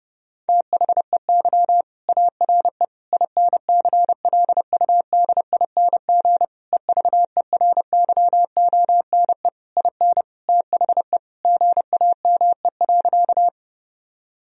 Morse code, heard as a keyed tone, about 30 wpm, 700 Hz. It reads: THEY ARE INCLUDING EVERYONE IN THE GAME.
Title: Sentences from Top 1000 Words - Review Part 4 - 30wpm